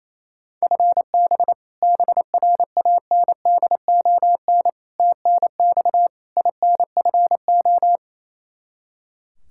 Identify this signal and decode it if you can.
Morse code, heard as a keyed tone, about 27 wpm, 700 Hz. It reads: FB BRANDON TNX INFO
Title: Practice QSO - Exchange 3 - More Info - 28wpm